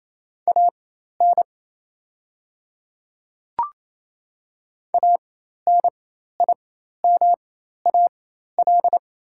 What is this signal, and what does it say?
Morse code, heard as a keyed tone, about 27 wpm, spaced as for 13 wpm, 690 Hz. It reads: AN ANIMAL
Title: 4x Character Spacing - Top 200 Words - 28wpm